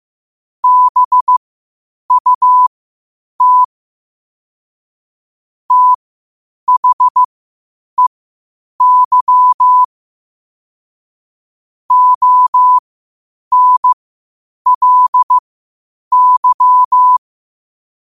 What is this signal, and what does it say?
Morse code, heard as a keyed tone, about 15 wpm, spaced as for 8 wpm, 1000 Hz. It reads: BUT THEY ONLY
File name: qotdfiles / 8wpm